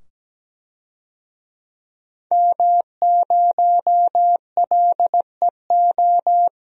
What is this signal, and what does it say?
Morse code, heard as a keyed tone, about 17 wpm, 700 Hz. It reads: M0LEO
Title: DXCC-100 Call Signs - 5 Characters - 17wpm